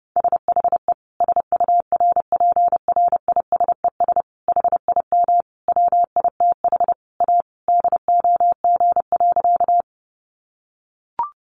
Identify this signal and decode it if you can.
Morse code, heard as a keyed tone, about 30 wpm, 700 Hz. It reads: SHE SURPRISES HIM WITH A DOG.
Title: Additional Easy Phrases 30wpm